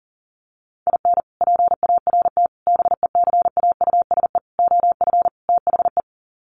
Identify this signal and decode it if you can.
Morse code, heard as a keyed tone, about 40 wpm, 700 Hz. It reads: IN PART BECAUSE OF THE
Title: Sets of 5 Words Encore2 40wpm